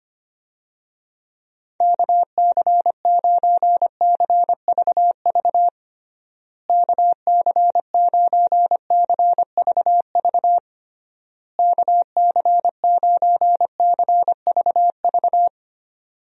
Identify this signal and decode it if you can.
Morse code, heard as a keyed tone, about 25 wpm, 700 Hz. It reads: KC9CVV KC9CVV KC9CVV
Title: USA Call Signs - 6 Characters - Repeated 3x - 25wpm